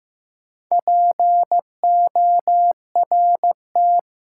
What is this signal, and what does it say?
Morse code, heard as a keyed tone, about 15 wpm, 700 Hz. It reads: PORT
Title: Sentences from Top 1000 Words - Review Part 1 - New Word Drill - 15wpm